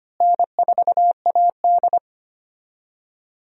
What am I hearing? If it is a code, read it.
Morse code, heard as a keyed tone, about 25 wpm, 700 Hz. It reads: N4AD
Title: USA Call Signs - 4 Characters - Repeated 3x - 25wpm